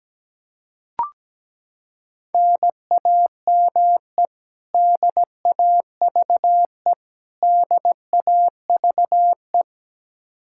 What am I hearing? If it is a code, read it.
Morse code, heard as a keyed tone, about 17 wpm, 700 Hz. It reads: NAME DAVE DAVE
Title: Practice QSO - Exchange 1 - Beginning - 17wpm